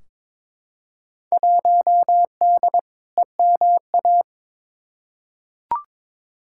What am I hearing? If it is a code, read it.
Morse code, heard as a keyed tone, about 22 wpm, 700 Hz. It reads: 1D EMA
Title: Field Day Exchanges - 22wpm